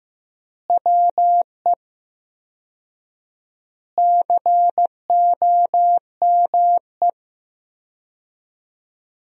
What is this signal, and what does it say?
Morse code, heard as a keyed tone, about 16 wpm, 700 Hz. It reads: WE COME
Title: 4x Word-Spacing - Sentences from Top 100 Words - 15wpm